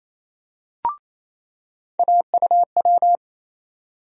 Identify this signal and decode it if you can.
Morse code, heard as a keyed tone, about 28 wpm, 700 Hz. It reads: AUW
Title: USA Call Sign Postfixes - 3 Letters 28wpm